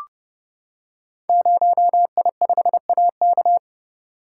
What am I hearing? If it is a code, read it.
Morse code, heard as a keyed tone, about 30 wpm, 700 Hz. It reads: 0I5AK
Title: Code Groups - 5 Letter:Numbers - 30wpm